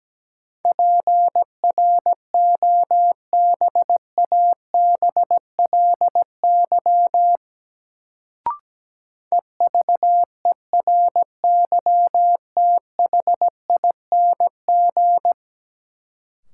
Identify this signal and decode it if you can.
Morse code, heard as a keyed tone, about 17 wpm, 700 Hz. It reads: PROBABLY EVERYTHING
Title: Sentences from Top 700 Words - New Word Drill - 17wpm